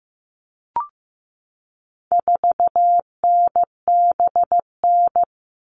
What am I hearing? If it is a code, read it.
Morse code, heard as a keyed tone, about 15 wpm, 700 Hz. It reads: 4NBN
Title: All Cut Numbers - 4-Digit 15wpm